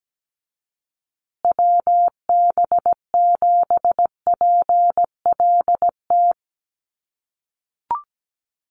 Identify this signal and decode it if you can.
Morse code, heard as a keyed tone, about 17 wpm, 700 Hz. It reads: WB7PLT
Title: USA Call Signs Repeated 3x 17wpm